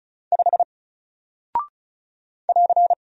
Morse code, heard as a keyed tone, about 36 wpm, 700 Hz. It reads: 5 <AR>
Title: Single Character (Letter:Number:Punctuation:Prosign) 35wpm